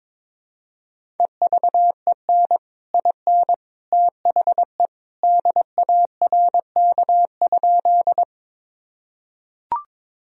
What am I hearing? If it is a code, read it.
Morse code, heard as a keyed tone, about 22 wpm, 700 Hz. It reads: EVEN IN THE DARK?
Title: Sentences from Top 1000 Words - Review Part 1 - 22wpm